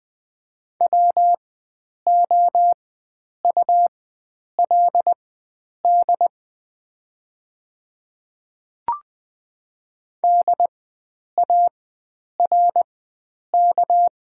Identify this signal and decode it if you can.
Morse code, heard as a keyed tone, about 20 wpm, spaced as for 9 wpm, 690 Hz. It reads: WOULD DARK
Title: 4x Character Spacing - Top 500 Words - 20wpm